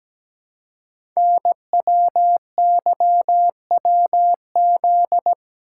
Morse code, heard as a keyed tone, about 17 wpm, 700 Hz. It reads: NWYWZ
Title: Code Groups - 5 Letters - 17wpm